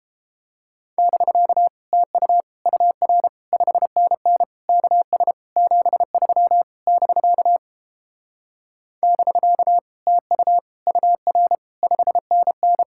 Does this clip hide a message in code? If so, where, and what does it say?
Yes: Morse code from 1.0 to 12.8 s, heard as a keyed tone, about 31 wpm, 700 Hz. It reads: <BK> TU UR 5NN KS 73 <BK> <BK> TU UR 5NN